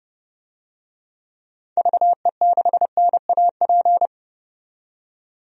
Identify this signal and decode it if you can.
Morse code, heard as a keyed tone, about 32 wpm, 700 Hz. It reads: VE6NAP
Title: DXCC-100 Call Signs - 6 Characters - 30wpm